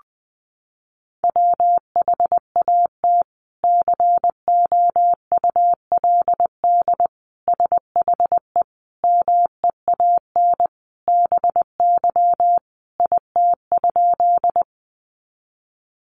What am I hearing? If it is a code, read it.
Morse code, heard as a keyed tone, about 20 wpm, 700 Hz. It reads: WHAT COULD SHE MEAN BY IT?
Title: Sentences from Top 200 Words 20wpm